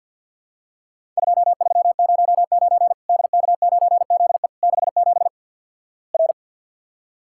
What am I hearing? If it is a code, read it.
Morse code, heard as a keyed tone, about 50 wpm, 700 Hz. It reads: 2309 DK9ZE 67 R
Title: Worked All Europe DX Contest - QTC Exchange - 50wpm